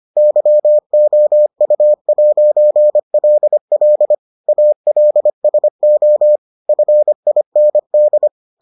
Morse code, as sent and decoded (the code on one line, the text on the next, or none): -.-- --- ..- .----. .-.. .-.. / .- .-.. ... --- / ..-. .. -. -..
YOU'LL ALSO FIND